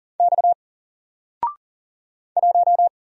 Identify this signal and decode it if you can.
Morse code, heard as a keyed tone, about 41 wpm, 700 Hz. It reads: X 1
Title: Single Letter-Number Alt 40wpm